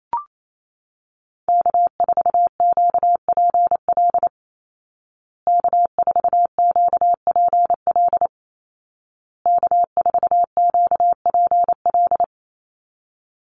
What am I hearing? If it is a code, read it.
Morse code, heard as a keyed tone, about 28 wpm, 700 Hz. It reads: K4QPL K4QPL K4QPL
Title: Call Signs Repeated 2x - 28wpm